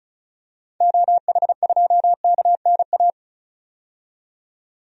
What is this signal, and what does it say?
Morse code, heard as a keyed tone, about 34 wpm, 700 Hz. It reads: OH2KNA
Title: DXCC-100 Call Signs - 6 Characters - 35wpm